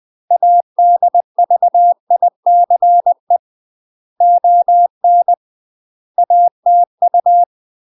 Morse code, heard as a keyed tone, about 20 wpm, 700 Hz. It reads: ADVICE ON ATU
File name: UK MASTER QSOs TRAINING / 20WPM / UK MASTER QSOs 0018